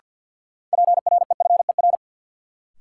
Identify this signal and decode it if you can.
Morse code, heard as a keyed tone, about 51 wpm, 700 Hz. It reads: PREFER
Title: Sentences from Top 1100 Words - New Word Drill Alt - 50wpm